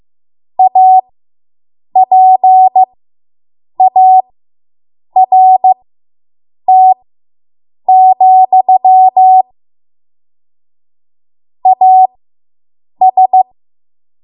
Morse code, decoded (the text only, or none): APART, AS